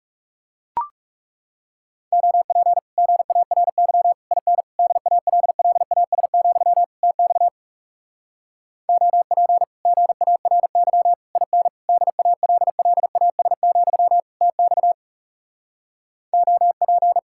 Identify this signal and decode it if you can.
Morse code, heard as a keyed tone, about 40 wpm, 700 Hz. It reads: OP GARY IN DALLAS, TX OP GARY IN DALLAS, TX OP